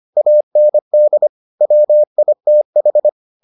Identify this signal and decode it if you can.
Morse code, heard as a keyed tone, about 25 wpm, 600 Hz. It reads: AND WITH